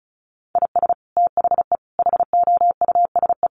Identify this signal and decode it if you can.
Morse code, heard as a keyed tone, about 34 wpm, 700 Hz. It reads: IS THE HOUSE
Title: Sentences from Top 1400 Words 35wpm